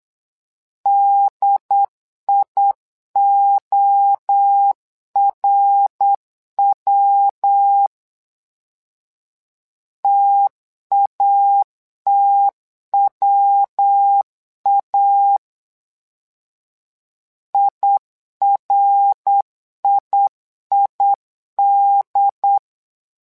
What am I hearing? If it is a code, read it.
Morse code, heard as a keyed tone, about 8 wpm, 800 Hz. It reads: DIORW TATWA IRIID
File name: LEZIONI / 3 MWRO / 3 MWRO 8wpm 2606